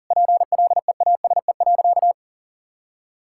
Morse code, heard as a keyed tone, about 40 wpm, 700 Hz. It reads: PLEASE.
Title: Sentences from Top 1600 Words 40wpm